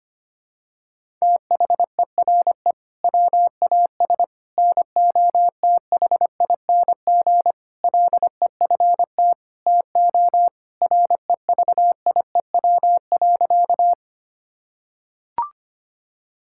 Morse code, heard as a keyed tone, about 25 wpm, 700 Hz. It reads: THERE WAS NOTHING LEFT TO REVIEW.
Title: Sentences from Top 1000 Words - Review Part 6 - 25wpm